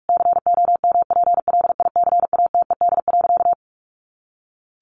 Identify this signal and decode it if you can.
Morse code, heard as a keyed tone, about 44 wpm, 700 Hz. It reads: COMPLICATED.
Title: Sentences from Top 1200 Words 45wpm